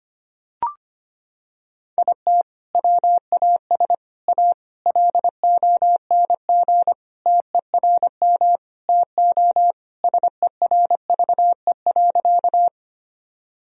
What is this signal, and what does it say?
Morse code, heard as a keyed tone, about 25 wpm, 700 Hz. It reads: IT WAS A LONG TERM TO SERVE.
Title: Sentences from Top 1000 Words - Review Part 4 - 25wpm